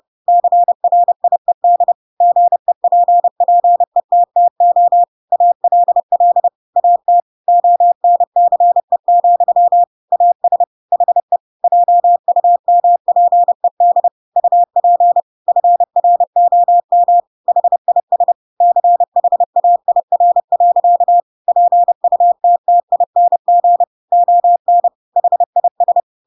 Morse code, heard as a keyed tone, about 30 wpm, 700 Hz. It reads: CRIED GEPPETTO ALL AT ONCE, AS HE JUMPED UP FROM HIS CHAIR. PUTTING ON HIS
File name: Pinocchio / Pinocchio ch8 30 30 ik1hge